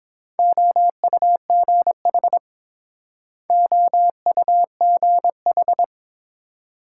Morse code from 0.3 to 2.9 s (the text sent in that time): OUGH